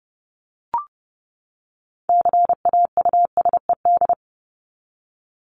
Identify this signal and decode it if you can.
Morse code, heard as a keyed tone, about 30 wpm, 700 Hz. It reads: CAUSED
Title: ED Suffix - Top 100 Words with Base Words 30wpm